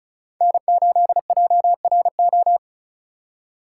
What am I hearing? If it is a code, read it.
Morse code, heard as a keyed tone, about 44 wpm, 700 Hz. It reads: N8JRO